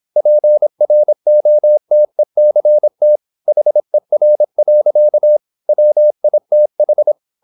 Morse code, heard as a keyed tone, about 26 wpm, 600 Hz. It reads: PROTECT HER. WITH
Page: Never 26wpm | CW med Gnister